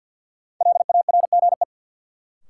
Morse code, heard as a keyed tone, about 53 wpm, 700 Hz. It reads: LARGE
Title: Sentences from Top 200 Words - New Word Drill - 50wpm